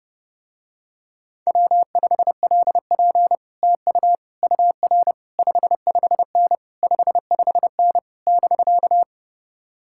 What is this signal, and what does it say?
Morse code, heard as a keyed tone, about 30 wpm, 700 Hz. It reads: W5LP TU UR 55N 55N <BK>